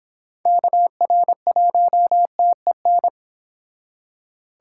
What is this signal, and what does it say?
Morse code, heard as a keyed tone, about 26 wpm, 700 Hz. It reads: KR1TEN